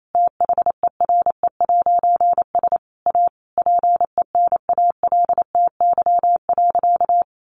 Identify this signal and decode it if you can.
Morse code, heard as a keyed tone, about 28 wpm, 700 Hz. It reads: THERE'S A PENALTY.
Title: A Sound Of Thunder 28wpm